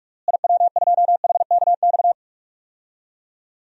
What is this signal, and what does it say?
Morse code, heard as a keyed tone, about 46 wpm, 700 Hz. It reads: IW2HKX
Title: DXCC-100 Call Signs - 6 Characters - Repeated 3x - 45wpm